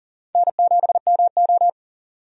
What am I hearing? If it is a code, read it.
Morse code, heard as a keyed tone, about 37 wpm, 700 Hz. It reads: N7MO
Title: USA Call Signs - 4 Characters - 40wpm